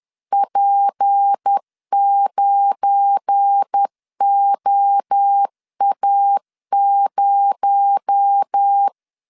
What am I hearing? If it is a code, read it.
Morse code, heard as a keyed tone, about 10 wpm, 800 Hz. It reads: P9OA0
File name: LEZIONI / 7 GF9Z / 7 GF9 10wpm 2747